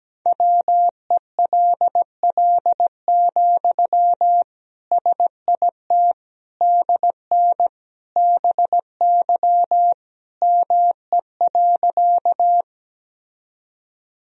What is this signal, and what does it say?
Morse code, heard as a keyed tone, about 17 wpm, 700 Hz. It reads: WELL, SIT DN BY ME.
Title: Sentences from Top 500 Words and Abbreviations 17wpm